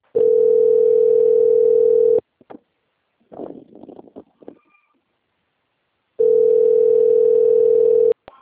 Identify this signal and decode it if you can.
Ringback tone (North America), 440 + 480 Hz.